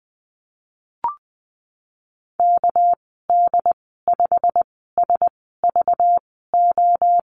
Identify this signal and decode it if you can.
Morse code, heard as a keyed tone, about 20 wpm, spaced as for 14 wpm, 700 Hz. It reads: KD5SVO
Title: 200 Lesson 32g - Letter k - Call Signs 20wpm